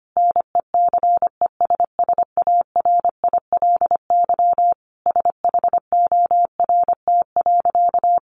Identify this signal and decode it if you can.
Morse code, heard as a keyed tone, about 25 wpm, 700 Hz. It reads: NECESSARILY SHORT.
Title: Sentences from Top 1400 Words 25wpm